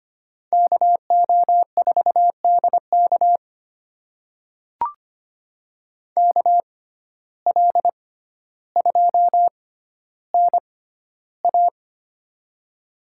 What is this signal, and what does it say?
Morse code, heard as a keyed tone, about 25 wpm, spaced as for 13 wpm, 700 Hz. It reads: KO4DK KL2NA
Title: USA Call Signs - 5 Characters - 6x Character Spacing - 25wpm